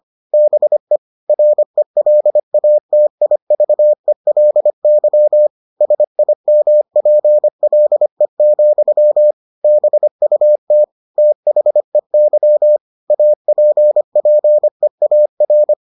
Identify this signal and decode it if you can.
Morse code, heard as a keyed tone, about 25 wpm, 600 Hz. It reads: BE RELATIVELY SIMPLE, BUT THEY APPEAR